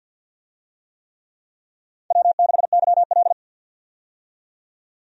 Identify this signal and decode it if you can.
Morse code, heard as a keyed tone, about 51 wpm, 700 Hz. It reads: W6YL
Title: USA Call Signs - 4 Characters - 50wpm